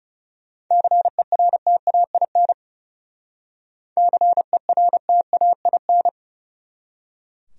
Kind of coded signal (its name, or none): Morse code